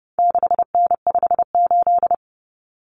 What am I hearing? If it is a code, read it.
Morse code, heard as a keyed tone, about 30 wpm, 700 Hz. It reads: 6N58
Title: Common Cut Numbers - 4-Digit 30wpm